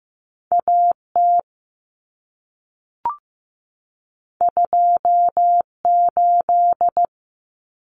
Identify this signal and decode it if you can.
Morse code, heard as a keyed tone, about 15 wpm, 700 Hz. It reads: AT 28
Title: Common Cut Numbers - 2-Digit 15wpm